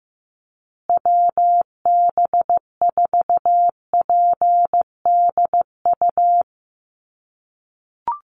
Morse code, heard as a keyed tone, about 15 wpm, 700 Hz. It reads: WB4PDU
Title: USA Call Signs Repeated 2x 15wpm